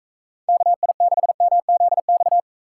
Morse code, heard as a keyed tone, about 41 wpm, 700 Hz. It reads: KI6MZX